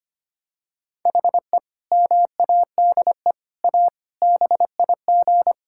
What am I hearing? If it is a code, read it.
Morse code, heard as a keyed tone, about 25 wpm, 700 Hz. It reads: HE MADE A BIG
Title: Sentences from Top 1000 Words - Review Part 6 - 25wpm